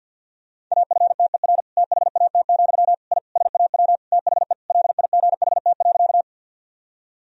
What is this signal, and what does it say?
Morse code, heard as a keyed tone, about 49 wpm, 700 Hz. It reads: AFTER THAT, I SAW THE LIGHT.
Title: Sentences from Top 1000 Words - Review Part 1 - 50wpm